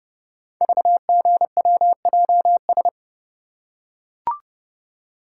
Morse code, heard as a keyed tone, about 30 wpm, 690 Hz. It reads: VGWJS
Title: Code Groups - 5 Letters - 30wpm